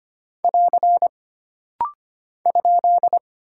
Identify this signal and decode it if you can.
Morse code, heard as a keyed tone, about 25 wpm, 700 Hz. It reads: <AR> ?
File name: Single Punctuation-Prosign - Rapid-Fire - 25wpm